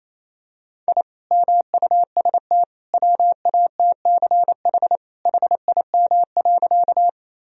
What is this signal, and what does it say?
Morse code, heard as a keyed tone, about 28 wpm, 700 Hz. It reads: I MUST WATCH HIM.
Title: Sentences from Top 1000 Words - Review Part 1 - 28wpm